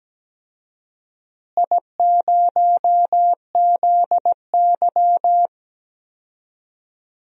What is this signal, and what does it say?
Morse code, heard as a keyed tone, about 17 wpm, 700 Hz. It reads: I0ZY
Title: DXCC-100 Call Signs - 4 Characters - Repeated 2x - 17wpm